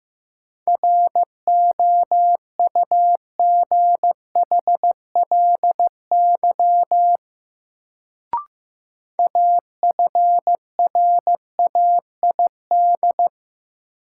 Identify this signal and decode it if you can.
Morse code, heard as a keyed tone, about 15 wpm, 700 Hz. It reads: ROUGHLY AFRAID
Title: Sentences from Top 1500 Words - New Word Drill Alt - 15wpm